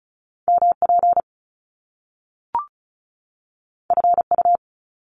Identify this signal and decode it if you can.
Morse code, heard as a keyed tone, about 36 wpm, 700 Hz. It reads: MP FU
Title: Top Two Letter Combinations in Words 35wpm